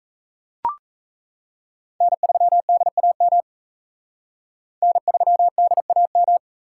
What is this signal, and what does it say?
Morse code, heard as a keyed tone, about 35 wpm, 700 Hz. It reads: N3DAM N3DAM
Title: Speed Racing - USA 1x3 Call Signs 28wpm